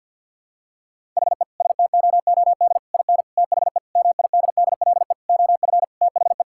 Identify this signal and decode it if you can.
Morse code, heard as a keyed tone, about 52 wpm, 700 Hz. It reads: HE STOOD IN THE MIDDLE OF THE
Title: Sentences from Top 1000 Words - Review Part 6 - 50wpm